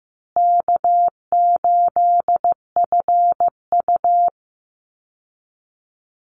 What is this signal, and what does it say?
Morse code, heard as a keyed tone, about 15 wpm, 700 Hz. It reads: K8FU
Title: Easy Call Signs Repeated 3x - 15wpm